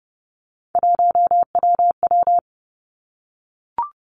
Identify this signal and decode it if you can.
Morse code, heard as a keyed tone, about 30 wpm, 700 Hz. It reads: 1WW
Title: Three Challenging Letter-Number Combinations 30wpm